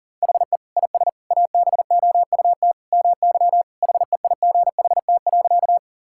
Morse code, heard as a keyed tone, about 40 wpm, 700 Hz. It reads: HE IS ABOUT MY HEIGHT.
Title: Sentences from Top 1600 Words 40wpm